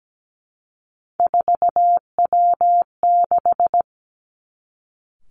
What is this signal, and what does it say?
Morse code, heard as a keyed tone, about 17 wpm, 700 Hz. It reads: 4W6
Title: Three Challenging Letter-Number Combinations 17wpm